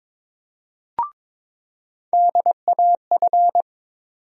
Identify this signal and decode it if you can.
Morse code, heard as a keyed tone, about 22 wpm, 700 Hz. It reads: DAF
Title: USA Call Sign Postfixes - 3 Letters 22wpm